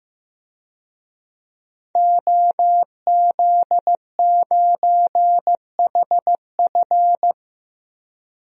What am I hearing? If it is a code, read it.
Morse code, heard as a keyed tone, about 16 wpm, 700 Hz. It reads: OZ9HF
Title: DXCC-100 Call Signs - 5 Characters - Repeated 3x - 15wpm